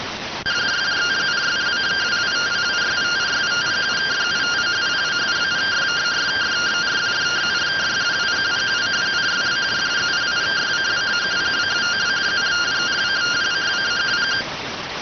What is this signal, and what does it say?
Two-tone teleprinter signal, about 170 Hz shift.